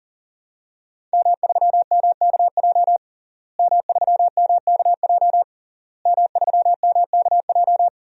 Morse code, heard as a keyed tone, about 40 wpm, 700 Hz. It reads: M3MKJ M3MKJ M3MKJ
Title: DXCC-100 Call Signs - 5 Characters - Repeated 3x - 40wpm